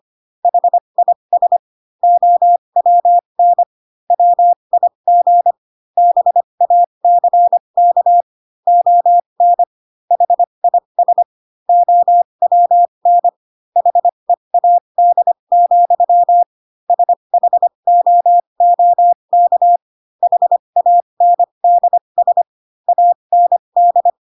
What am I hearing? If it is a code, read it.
Morse code, heard as a keyed tone, about 25 wpm, spaced as for 22 wpm, 700 Hz. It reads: HIS OWN WIG BACK ON HIS OWN HEAD, SHOOK HANDS AND
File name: Pinocchio / Pinocchio ch2 18 25 ik1hge